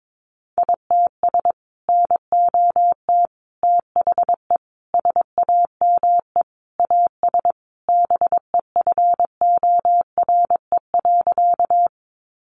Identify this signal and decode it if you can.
Morse code, heard as a keyed tone, about 22 wpm, 700 Hz. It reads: ITS NOT THE SAME AS BEFORE.
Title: Sentences from Top 600 Words 22wpm